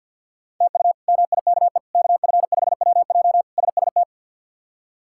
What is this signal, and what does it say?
Morse code, heard as a keyed tone, about 48 wpm, 700 Hz. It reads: TU MIKE KF5WJ SST
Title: K1USN SST QSO Practice - 50wpm